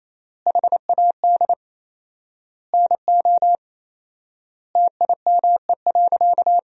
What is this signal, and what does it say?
Morse code, heard as a keyed tone, about 28 wpm, 700 Hz. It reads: HAD NO TIME.
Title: 4x Word-Spacing - Sentences from Top 100 Words - 28wpm